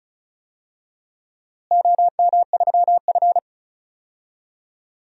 Morse code, heard as a keyed tone, about 36 wpm, 700 Hz. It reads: OM3F
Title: DXCC-100 Call Signs - 4 Characters - Repeated 2x - 35wpm